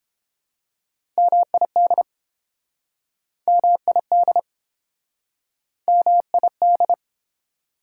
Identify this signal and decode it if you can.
Morse code, heard as a keyed tone, about 31 wpm, 690 Hz. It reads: MID MID MID